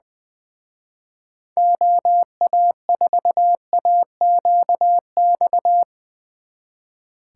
Morse code, heard as a keyed tone, about 22 wpm, 700 Hz. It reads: OA4AQX
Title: Speed Racing - USA 2x3 Call Signs 20wpm